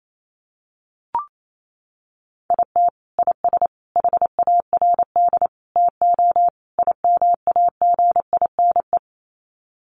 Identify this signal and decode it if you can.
Morse code, heard as a keyed tone, about 28 wpm, 700 Hz. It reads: IT IS HARD TO IMAGINE
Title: Sets of 5 Words Encore2 28wpm